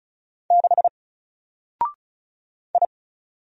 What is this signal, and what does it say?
Morse code, heard as a keyed tone, about 36 wpm, 700 Hz. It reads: 6 I
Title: Single Character (Letter:Number:Punctuation:Prosign) Alt 35wpm